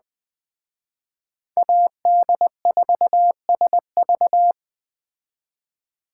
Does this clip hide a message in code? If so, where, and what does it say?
Yes: Morse code from 1.6 to 4.5 s, heard as a keyed tone, about 20 wpm, 700 Hz. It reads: AD4SV